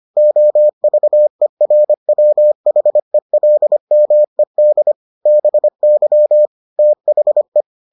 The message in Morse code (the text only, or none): OVERWHELMED BY THE